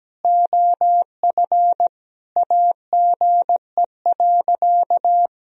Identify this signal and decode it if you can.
Morse code, heard as a keyed tone, about 17 wpm, 700 Hz. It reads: OF AGE.
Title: Sentences from Top 1000 Words - Review Part 3 - 17wpm